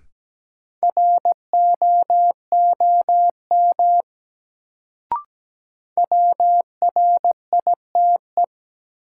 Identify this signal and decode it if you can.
Morse code, heard as a keyed tone, about 17 wpm, 700 Hz. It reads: ROOM WRITE
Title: Sentences from Top 300 Words - New Word Drill - 17wpm